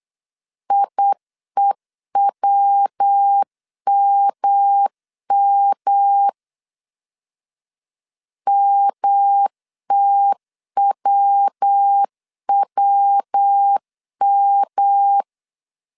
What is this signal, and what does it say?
Morse code, heard as a keyed tone, about 8 wpm, 800 Hz. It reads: IEWMM MTWWM
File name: CW / DET / 1 3 1500C S B 8wpm 4141